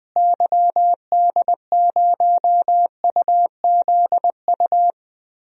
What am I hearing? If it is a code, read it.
Morse code, heard as a keyed tone, about 20 wpm, 700 Hz. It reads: YD0UZU
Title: DXCC-100 Call Signs - 6 Characters - Repeated 3x - 20wpm